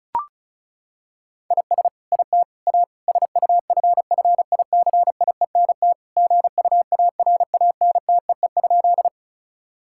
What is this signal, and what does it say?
Morse code, heard as a keyed tone, about 36 wpm, 700 Hz. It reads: IS IT A SUFFICIENT GUARANTEE?
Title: Sentences from Top 1600 Words 35wpm